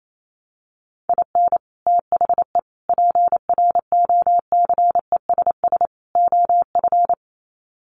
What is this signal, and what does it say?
Morse code, heard as a keyed tone, about 28 wpm, 700 Hz. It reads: IN THE PROCESS OF
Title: Sets of 4 Words 28wpm